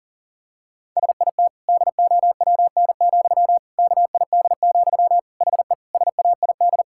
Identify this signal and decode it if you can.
Morse code, heard as a keyed tone, about 40 wpm, 700 Hz. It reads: SIT DOWN, KID, HE SAID
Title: Sentences from Top 1600 Words 40wpm